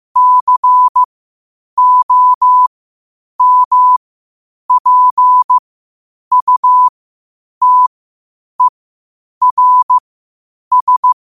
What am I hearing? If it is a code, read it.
Morse code, heard as a keyed tone, about 15 wpm, spaced as for 8 wpm, 1000 Hz. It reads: COMPUTERS
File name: qotdfiles / 8wpm